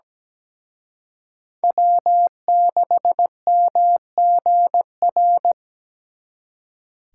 Morse code, heard as a keyed tone, about 20 wpm, 700 Hz. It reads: W6MGR
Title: Speed Racing - USA 1x3 Call Signs 17wpm